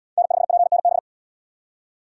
Morse code, heard as a keyed tone, about 68 wpm, 700 Hz. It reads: N5PNB